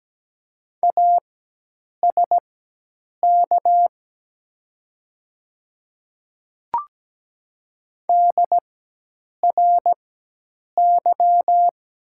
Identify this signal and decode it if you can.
Morse code, heard as a keyed tone, about 17 wpm, spaced as for 8 wpm, 700 Hz. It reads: ASK DRY
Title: 4x Character Spacing - Top 500 Words - 17wpm